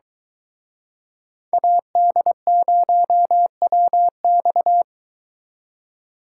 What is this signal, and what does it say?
Morse code, heard as a keyed tone, about 23 wpm, 700 Hz. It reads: AD0WX